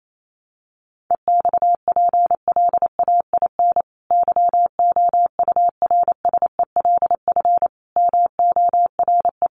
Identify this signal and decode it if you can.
Morse code, heard as a keyed tone, about 28 wpm, 700 Hz. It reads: EXPLAIN YOURSELF MORE
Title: Sentences from Top 1300 Words 28wpm